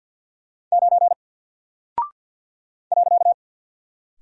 Morse code, heard as a keyed tone, about 48 wpm, 700 Hz. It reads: 9 .
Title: Single Character (Letter:Number:Punctuation:Prosign) Alt 50wpm